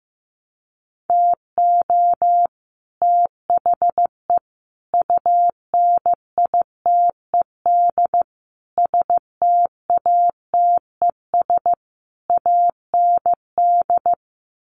Morse code, heard as a keyed tone, about 14 wpm, 700 Hz. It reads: TO THE UNITED STATES AND
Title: Sets of 5 Words Encore 15wpm